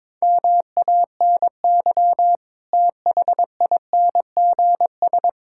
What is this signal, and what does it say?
Morse code, heard as a keyed tone, about 22 wpm, 700 Hz. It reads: MANY THINGS
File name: Sentences with 7-Word Phrases - 22wpm